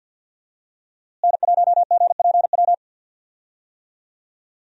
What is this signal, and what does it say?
Morse code, heard as a keyed tone, about 51 wpm, 700 Hz. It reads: N1GPW
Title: USA Call Signs - 5 Characters - Repeated 2x - 50wpm